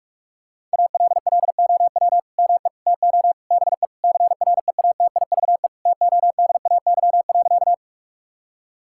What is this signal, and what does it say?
Morse code, heard as a keyed tone, about 46 wpm, 700 Hz. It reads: ALLOW ME TO BE CREATIVE TODAY.
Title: Sentences from Top 1000 Words - Review Part 5 - 45wpm